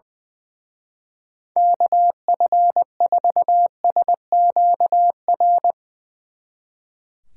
Morse code, heard as a keyed tone, about 21 wpm, 700 Hz. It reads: KF4SQR
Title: Speed Racing - USA 2x3 Call Signs 20wpm